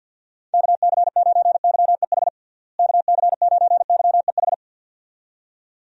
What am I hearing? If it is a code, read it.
Morse code, heard as a keyed tone, about 51 wpm, 700 Hz. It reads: KC9YEH KC9YEH
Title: USA Call Signs - 6 Characters - Repeated 2x - 50wpm